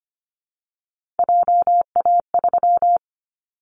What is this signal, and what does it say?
Morse code, heard as a keyed tone, about 25 wpm, 700 Hz. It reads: JA3
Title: Callsign Prefixes for Top 25 DXCC - 25wpm